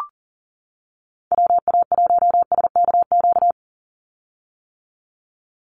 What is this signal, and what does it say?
Morse code, heard as a keyed tone, about 37 wpm, 700 Hz. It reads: WA1SKQ
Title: Call Signs - 40wpm